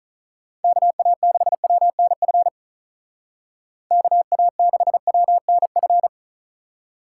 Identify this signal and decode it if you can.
Morse code, heard as a keyed tone, about 43 wpm, 700 Hz. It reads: KA6WNF KA6WNF
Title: Speed Racing - USA 2x3 Call Signs 35wpm